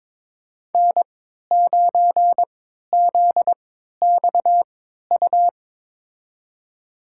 Morse code, heard as a keyed tone, about 22 wpm, spaced as for 12 wpm, 700 Hz. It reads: N9ZXU
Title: USA Call Signs - 5 Characters - 3x Character Spacing - 22wpm